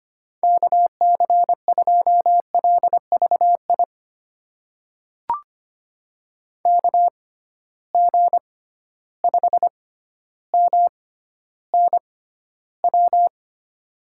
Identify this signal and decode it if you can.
Morse code, heard as a keyed tone, about 25 wpm, spaced as for 13 wpm, 700 Hz. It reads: KC2LVI KG5MNW
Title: USA Call Signs - 6 Characters - 6x Character Spacing - 25wpm